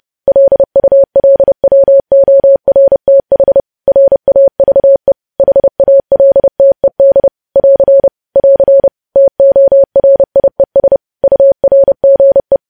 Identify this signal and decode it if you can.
Morse code, heard as a keyed tone, about 30 wpm, 550 Hz. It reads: LULWORTH RAVE HALTED <AR> <AR> TORIES URGE